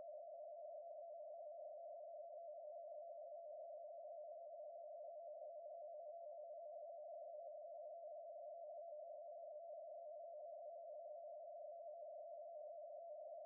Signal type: two-tone teleprinter signal